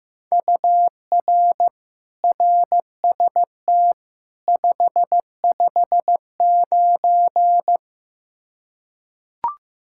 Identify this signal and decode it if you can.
Morse code, heard as a keyed tone, about 15 wpm, 700 Hz. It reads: UR RST 559
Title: Practice QSO - Exchange 1 - Beginning - 15wpm